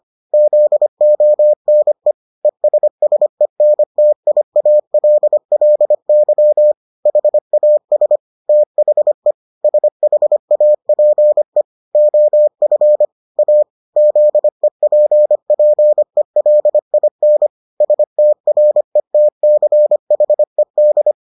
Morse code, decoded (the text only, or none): ZONE ESSENTIALLY HAS THE SHAPE OF A ZEPPELIN STRETCHED